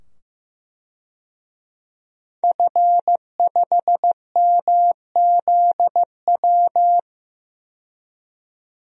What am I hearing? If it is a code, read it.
Morse code, heard as a keyed tone, about 15 wpm, 700 Hz. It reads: F5MZW
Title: DXCC-100 Call Signs - 5 Characters - 15wpm